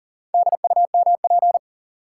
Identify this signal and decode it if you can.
Morse code, heard as a keyed tone, about 40 wpm, 700 Hz. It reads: DUMP
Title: Sentences from Top 1300 Words - New Word Drill Alt - 40wpm